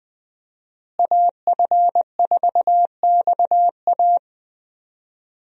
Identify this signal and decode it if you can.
Morse code, heard as a keyed tone, about 20 wpm, 700 Hz. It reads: AF4XA
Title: USA Call Signs - 5 Characters - Repeated 2x - 20wpm